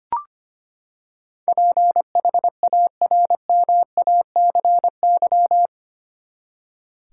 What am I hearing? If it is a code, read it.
Morse code, heard as a keyed tone, about 25 wpm, 700 Hz. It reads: PHARMACY
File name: Eight Letter Words - 25wpm